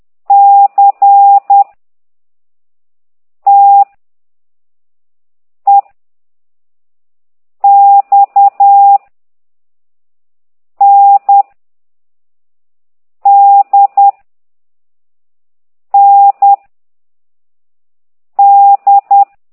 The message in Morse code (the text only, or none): C T E X N D N D